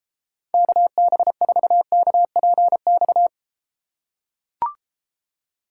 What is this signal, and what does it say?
Morse code, heard as a keyed tone, about 34 wpm, 700 Hz. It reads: KB4KPX